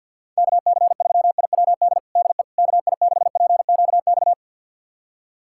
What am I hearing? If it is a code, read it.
Morse code, heard as a keyed tone, about 53 wpm, 690 Hz. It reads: KC3IWD DE KI6PQX
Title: USA Callsign DE Callsign - 6 Characters - 50wpm